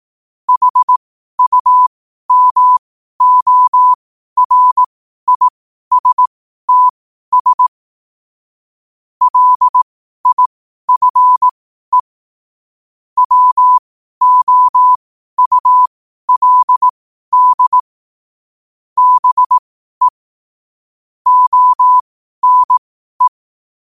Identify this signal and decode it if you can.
Morse code, heard as a keyed tone, about 18 wpm, spaced as for 12 wpm, 1000 Hz. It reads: HUMORISTS LIFE WOULD BE ONE
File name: qotdfiles / 12wpm